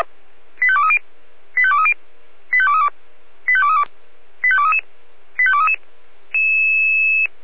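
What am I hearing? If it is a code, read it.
Five-tone selective call (ZVEI), heard as quick runs of five tones, tones of about 70 ms. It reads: 85230 85230 85232 85232 85233 85233
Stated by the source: ZVEI five-tone call received on an SDR.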